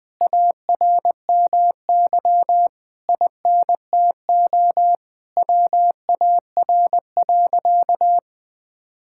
Morse code, heard as a keyed tone, about 20 wpm, 700 Hz. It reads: ARMY INTO WAR.
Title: Sentences from Top 1000 Words - Review Part 5 - 20wpm